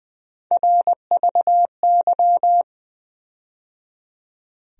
Morse code, heard as a keyed tone, about 20 wpm, 700 Hz. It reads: RVY
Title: USA Call Sign Postfixes - 3 Letters 20wpm